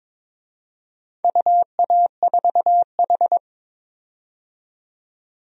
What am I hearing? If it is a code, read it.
Morse code, heard as a keyed tone, about 22 wpm, 700 Hz. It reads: UA4H
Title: DXCC-100 Call Signs - 4 Characters - 22wpm